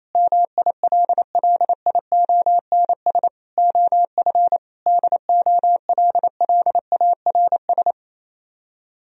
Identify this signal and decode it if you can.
Morse code, heard as a keyed tone, about 27 wpm, 700 Hz. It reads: MILLIONS OF DOLLARS
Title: Speed-Racing - Sets of 5 Words 28wpm